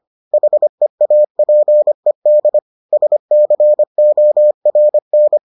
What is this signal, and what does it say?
Morse code, heard as a keyed tone, about 25 wpm, 600 Hz. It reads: HEAPED SCORN